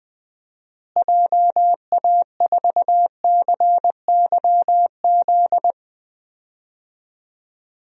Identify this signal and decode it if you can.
Morse code, heard as a keyed tone, about 20 wpm, 700 Hz. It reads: JA4CYZ
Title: DXCC-100 Call Signs - 6 Characters - Repeated 3x - 20wpm